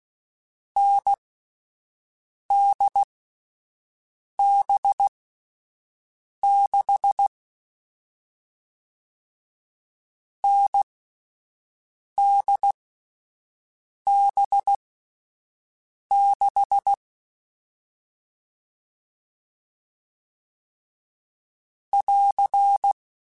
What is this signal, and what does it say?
Morse code, heard as a keyed tone, about 16 wpm, spaced as for 5 wpm, 800 Hz. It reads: NDB6 NDB6 <AR>
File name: Difficult Combos / Difficult Combos-mp3s / NDB NDB6 @5